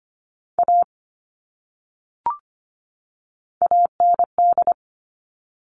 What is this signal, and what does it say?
Morse code, heard as a keyed tone, about 25 wpm, 700 Hz. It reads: A AND
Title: Top 500 Words 25wpm